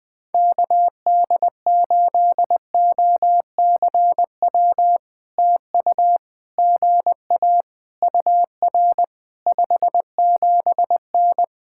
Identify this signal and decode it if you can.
Morse code, heard as a keyed tone, about 19 wpm, 700 Hz. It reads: KD8OCW TU GA UR 57N
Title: POTA QSO Practice - 20wpm